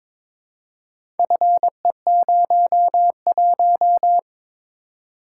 Morse code, heard as a keyed tone, about 23 wpm, 700 Hz. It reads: FE01
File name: practice / 22wpm / Four Letter-Number Combinations - 22wpm